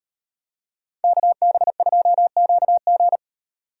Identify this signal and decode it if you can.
Morse code, heard as a keyed tone, about 38 wpm, 700 Hz. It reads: KB2QG